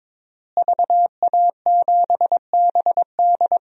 Morse code, heard as a keyed tone, about 22 wpm, 700 Hz. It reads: VA7BD
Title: DXCC-100 Call Signs - 5 Characters - Repeated 2x - 22wpm